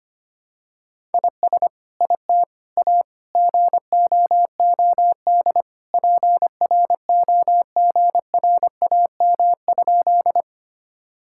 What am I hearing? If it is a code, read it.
Morse code, heard as a keyed tone, about 25 wpm, 700 Hz. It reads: IS IT A GOOD PROGRAM?
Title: Sentences from Top 1000 Words - Review Part 5 - 25wpm